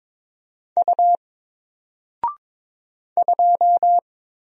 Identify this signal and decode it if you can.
Morse code, heard as a keyed tone, about 22 wpm, 700 Hz. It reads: U 2
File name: Single-Letter-Number-22wpm